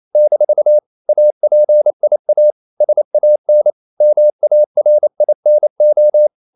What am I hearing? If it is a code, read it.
Morse code, heard as a keyed tone, about 28 wpm, 600 Hz. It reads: - APIA SAN MARINO
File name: CW / Practice / NationalCapitals28